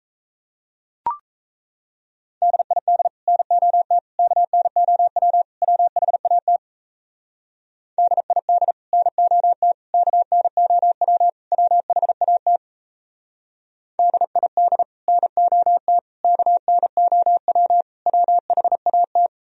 Morse code, heard as a keyed tone, about 37 wpm, 700 Hz. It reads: DID NOT KNOW WHAT DID NOT KNOW WHAT DID NOT KNOW WHAT